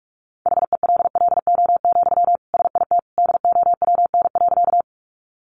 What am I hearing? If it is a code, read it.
Morse code, heard as a keyed tone, about 44 wpm, 700 Hz. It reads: HELLO, SIT DOWN.
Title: Sentences from Top 1100 Words 45wpm